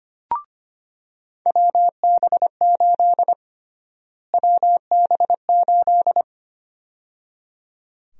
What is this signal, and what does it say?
Morse code, heard as a keyed tone, about 25 wpm, 700 Hz. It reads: WB8 WB8
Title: USA Call Sign Prefixes - Repeated 2x - 25wpm